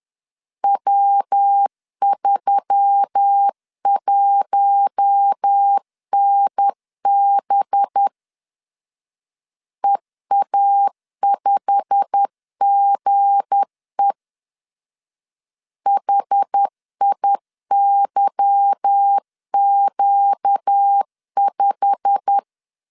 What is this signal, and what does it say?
Morse code, heard as a keyed tone, about 10 wpm, 800 Hz. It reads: W31NB EA5GE HIYQ5
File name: LEZIONI / 8 YJ45 / 8 YJ45 10wpm 2549